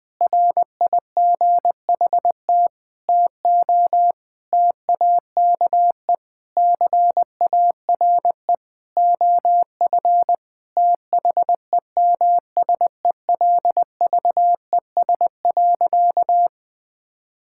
Morse code, heard as a keyed tone, about 20 wpm, 700 Hz. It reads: RIGHT TO TAKE CARE OF THEMSELVES.
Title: Sentences from Top 1000 Words - Review Part 5 - 20wpm